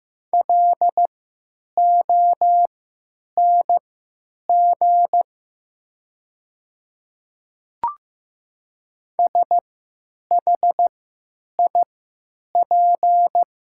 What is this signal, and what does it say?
Morse code, heard as a keyed tone, about 15 wpm, spaced as for 9 wpm, 700 Hz. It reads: LONG SHIP
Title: 3x Character Spacing - Top 500 Words - 15wpm